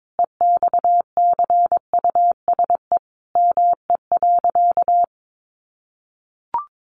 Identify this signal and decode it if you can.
Morse code, heard as a keyed tone, about 22 wpm, 700 Hz. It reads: EXCUSE ME.
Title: Sentences from Top 700 Words 22wpm